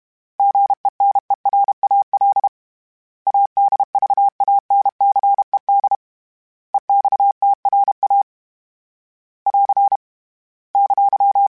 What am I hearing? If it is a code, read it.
Morse code, heard as a keyed tone, about 32 wpm, 800 Hz. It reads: GENERAL ADVANCED EXTRA <AR> !
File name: fromSMs / FromSMs mp3s / Classes@30